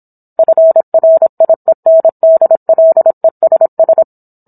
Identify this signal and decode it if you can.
Morse code, heard as a keyed tone, about 27 wpm, 650 Hz. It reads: FRIENDLESS